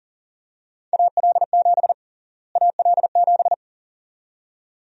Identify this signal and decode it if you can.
Morse code, heard as a keyed tone, about 40 wpm, 700 Hz. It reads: AL7 AL7
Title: USA Call Sign Prefixes - Repeated 2x - 40wpm